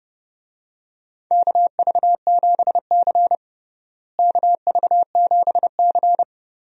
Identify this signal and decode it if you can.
Morse code, heard as a keyed tone, about 30 wpm, 700 Hz. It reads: KV7C KV7C
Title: USA Call Signs - 4 Characters - Repeated 3x - 30wpm